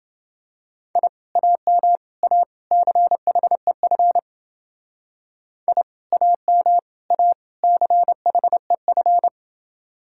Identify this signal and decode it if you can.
Morse code, heard as a keyed tone, about 28 wpm, 700 Hz. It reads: I AM A CHEF I AM A CHEF